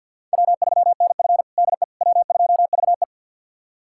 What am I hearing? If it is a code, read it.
Morse code, heard as a keyed tone, about 50 wpm, 700 Hz. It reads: W3NF DE W2VE
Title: USA Callsign DE Callsign - 4 Characters - 50wpm